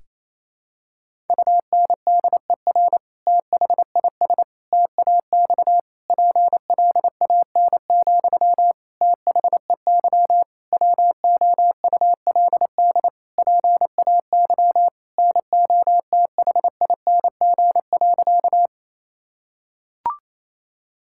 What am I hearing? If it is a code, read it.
Morse code, heard as a keyed tone, about 28 wpm, 700 Hz. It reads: UNDER THIS TAX PLAN, THEY WOULD PAY NOTHING.
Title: Sentences from Top 1000 Words - Review Part 5 - 28wpm